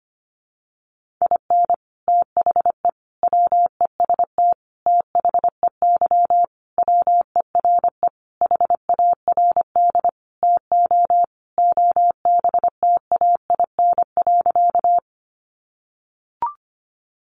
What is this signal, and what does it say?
Morse code, heard as a keyed tone, about 25 wpm, 700 Hz. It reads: IN THE WEST THEY WERE HARD TO OBTAIN.
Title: Sentences from Top 1300 Words 25wpm